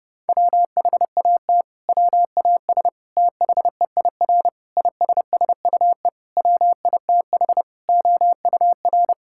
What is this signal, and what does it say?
Morse code, heard as a keyed tone, about 30 wpm, 700 Hz. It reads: WHAT WAS THEIR ISSUE WITH OUR
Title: Sentences from Top 1000 Words - Review Part 6 - 30wpm